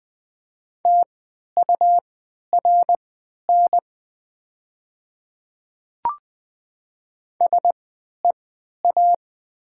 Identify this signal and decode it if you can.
Morse code, heard as a keyed tone, about 20 wpm, spaced as for 11 wpm, 700 Hz. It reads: TURN SEA
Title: 3x Character Spacing - Top 300 Words - 20wpm